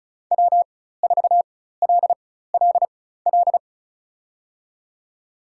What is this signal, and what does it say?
Morse code, heard as a keyed tone, about 34 wpm, spaced as for 16 wpm, 700 Hz. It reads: W4LLL
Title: USA Call Signs - 5 Characters - 4x Character Spacing - 35wpm